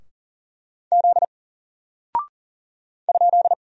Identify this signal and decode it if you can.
Morse code, heard as a keyed tone, about 40 wpm, 700 Hz. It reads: Z ?
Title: Single Character (Letter:Number:Punctuation:Prosign) Alt 40wpm